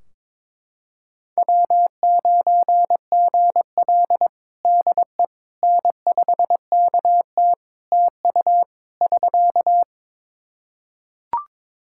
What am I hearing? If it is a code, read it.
Morse code, heard as a keyed tone, about 22 wpm, 700 Hz. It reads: W9GL DE N5KT TU <SK>
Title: Practice QSO - Exchange 4 - Ending - 22wpm